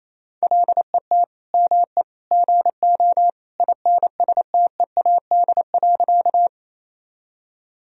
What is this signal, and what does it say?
Morse code, heard as a keyed tone, about 28 wpm, 700 Hz. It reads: LET ME GO INSTEAD.
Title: Sentences from Top 1000 Words - Review Part 4 - 28wpm